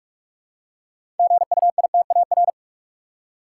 Morse code, heard as a keyed tone, about 46 wpm, 700 Hz. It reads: GUITAR
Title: Six Letter Words - 45wpm